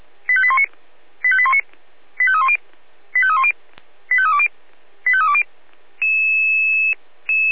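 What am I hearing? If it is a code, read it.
Five-tone selective call (ZVEI), heard as quick runs of five tones, tones of about 70 ms. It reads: 85819 85819 85310 85310 85320 85320
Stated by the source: ZVEI five-tone selective call received on SDR, from audio.f85360.feuerwehr-orlamuende.org (Feuerwehr Orlamünde fire service).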